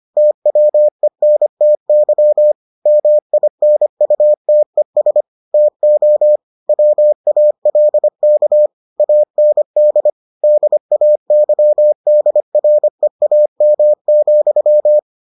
Morse code, heard as a keyed tone, about 25 wpm, 600 Hz. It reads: TWENTY MINUTES TO WALK AND DAYDREAM,